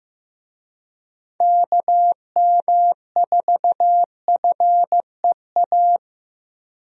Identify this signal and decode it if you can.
Morse code, heard as a keyed tone, about 15 wpm, 700 Hz. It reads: KM4FEA
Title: USA Call Signs - 6 Characters - Repeated 3x - 15wpm